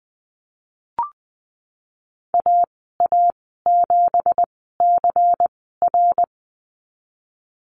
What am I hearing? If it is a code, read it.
Morse code, heard as a keyed tone, about 20 wpm, spaced as for 14 wpm, 700 Hz. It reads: AA7CR
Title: 167 Lesson 27g - Number 7 - Call Signs 20wpm